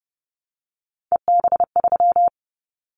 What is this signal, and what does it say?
Morse code, heard as a keyed tone, about 30 wpm, 700 Hz. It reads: EB3
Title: Callsign Prefixes for Top 25 DXCC - Repeated 3x - 30wpm